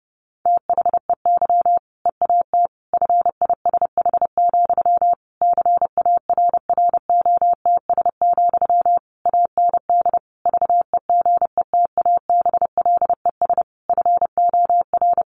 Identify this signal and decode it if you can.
Morse code, heard as a keyed tone, about 30 wpm, 700 Hz. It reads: THEY EAT FISH, CARROTS, AND VEGETABLES FOR
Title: Easy Phrases 30wpm